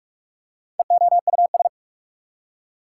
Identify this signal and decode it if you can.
Morse code, heard as a keyed tone, about 46 wpm, 690 Hz. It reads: EOUS
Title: Prefixes and Suffixes 45wpm